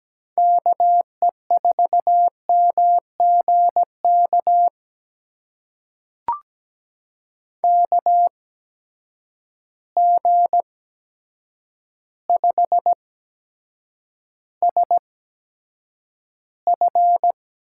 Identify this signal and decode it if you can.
Morse code, heard as a keyed tone, about 17 wpm, spaced as for 8 wpm, 700 Hz. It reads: KE4MGK KG5SF
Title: USA Call Signs - 6 Characters - 8x Character Spacing - 17wpm